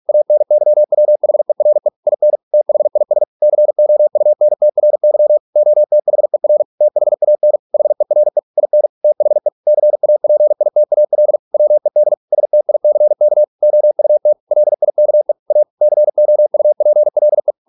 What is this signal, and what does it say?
Morse code, heard as a keyed tone, about 46 wpm, 600 Hz. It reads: ANYWHERE IN THIS COUNTRY OTHER THAN HERE IN THE CAPITAL WED STICK OUT LIKE A COUPLE